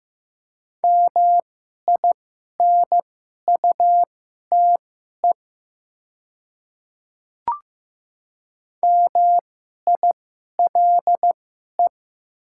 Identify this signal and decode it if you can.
Morse code, heard as a keyed tone, about 15 wpm, spaced as for 11 wpm, 700 Hz. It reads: MINUTE MILE
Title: 2x Character Spacing - Top 500 Words - 15wpm